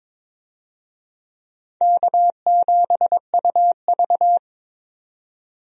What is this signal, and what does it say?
Morse code, heard as a keyed tone, about 22 wpm, 700 Hz. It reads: K7UV
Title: USA Call Signs - 4 Characters - 22wpm